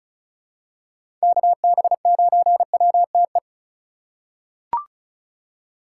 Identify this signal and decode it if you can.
Morse code, heard as a keyed tone, about 36 wpm, 700 Hz. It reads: KB9WTE
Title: USA Call Signs - 6 Characters - 4x Character Spacing - 35wpm